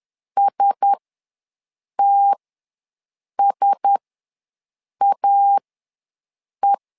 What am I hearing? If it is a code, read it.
Morse code, heard as a keyed tone, about 10 wpm, spaced as for 6 wpm, 800 Hz. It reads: STSAE
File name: LEZIONI / 4 Q1S ws 10wpm 2208